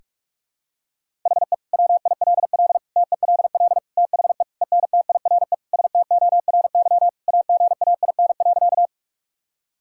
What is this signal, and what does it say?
Morse code, heard as a keyed tone, about 46 wpm, 700 Hz. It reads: HE WILL TELL THE ENTIRE STORY AGAIN.
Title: Sentences from Top 1000 Words - Review Part 5 - 45wpm